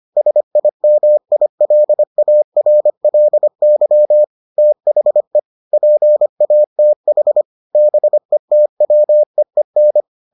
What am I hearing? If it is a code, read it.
Morse code, heard as a keyed tone, about 25 wpm, 600 Hz. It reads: SIMILARLY THE PATH BETWEEN